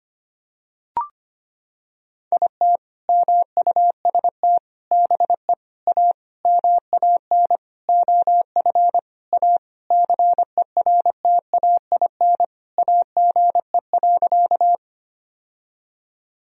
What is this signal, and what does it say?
Morse code, heard as a keyed tone, about 25 wpm, 700 Hz. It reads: IT MUST BE A MAN OF A CERTAIN AGE.
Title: Sentences from Top 1000 Words - Review Part 3 - 25wpm